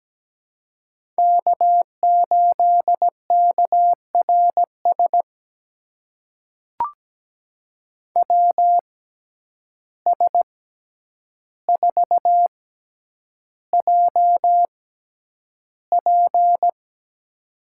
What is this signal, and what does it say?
Morse code, heard as a keyed tone, about 17 wpm, spaced as for 9 wpm, 700 Hz. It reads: K8KRS WS4JP
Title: USA Call Signs - 5 Characters - 6x Character Spacing - 17wpm